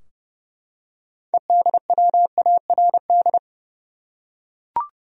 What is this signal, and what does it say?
Morse code, heard as a keyed tone, about 31 wpm, 700 Hz. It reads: EDWARD
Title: USA Common Names - 30wpm